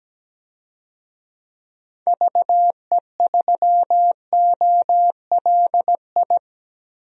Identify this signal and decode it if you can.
Morse code, heard as a keyed tone, about 17 wpm, 700 Hz. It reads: VE3OLI
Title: DXCC-100 Call Signs - 6 Characters - Repeated 2x - 17wpm